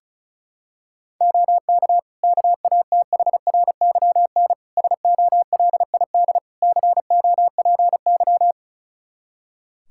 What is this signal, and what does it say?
Morse code, heard as a keyed tone, about 34 wpm, 700 Hz. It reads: OK KATHRYN SOLID COPY
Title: K1USN SST QSO Practice 35wpm